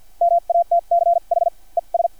Morse code, heard as a keyed tone, about 50 wpm, spaced as for 40 wpm, 690 Hz. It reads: MATCH ES
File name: archived / CRSnet